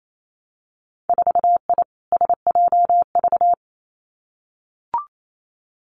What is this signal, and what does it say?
Morse code, heard as a keyed tone, about 28 wpm, 700 Hz. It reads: 4I SJV
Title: Winter Field Day Exchanges Encore 28wpm